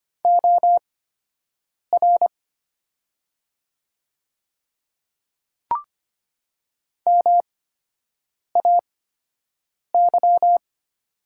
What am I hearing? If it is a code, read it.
Morse code, heard as a keyed tone, about 25 wpm, spaced as for 7 wpm, 700 Hz. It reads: OR MAY
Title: 8x Character Spacing - Top 100 Words - 25wpm